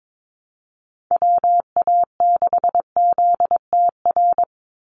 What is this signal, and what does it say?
Morse code, heard as a keyed tone, about 22 wpm, 700 Hz. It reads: WA6ZTR
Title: USA Call Signs 22wpm